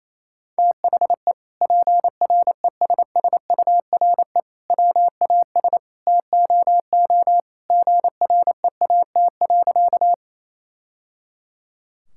Morse code, heard as a keyed tone, about 28 wpm, 700 Hz. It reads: THE PRESSURE WAS TOO GREAT.
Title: Sentences from Top 1000 Words - Review Part 6 - 28wpm